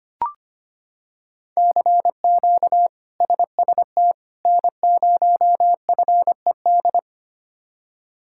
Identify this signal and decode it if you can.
Morse code, heard as a keyed tone, about 25 wpm, 700 Hz. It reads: CQ SST N0FED
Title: K1USN SST QSO Practice - 25wpm